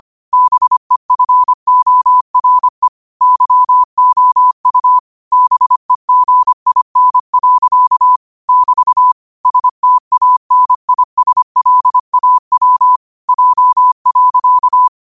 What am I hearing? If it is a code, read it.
Morse code, heard as a keyed tone, about 25 wpm, 1000 Hz. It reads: BEFORE YOU BEGIN. <BT> STANISLAW J.
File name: qotdfiles / 25wpm